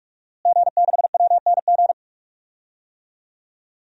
Morse code, heard as a keyed tone, about 46 wpm, 700 Hz. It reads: G6WNG
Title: DXCC-100 Call Signs - 5 Characters - Repeated 3x - 45wpm